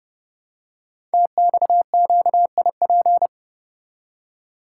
Morse code, heard as a keyed tone, about 30 wpm, 700 Hz. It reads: TXQIP
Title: Code Groups - 5 Letters - 30wpm